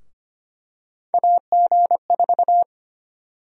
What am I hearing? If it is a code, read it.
Morse code, heard as a keyed tone, about 25 wpm, 700 Hz. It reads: AG4
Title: USA Call Sign Prefixes - Repeated 2x - 25wpm